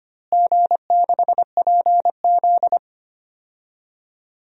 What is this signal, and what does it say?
Morse code, heard as a keyed tone, about 25 wpm, 700 Hz. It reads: G6PZ
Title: DXCC-100 Call Signs - 4 Characters - Repeated 3x - 25wpm